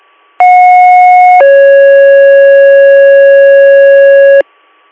Two-tone page, heard as a one-second tone, then a three-second tone, about 730 then 550 Hz. It is heard from 0.4 to 4.4 s.